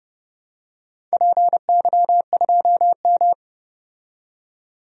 Morse code, heard as a keyed tone, about 30 wpm, 700 Hz. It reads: PY2M
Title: DXCC-100 Call Signs - 4 Characters - 30wpm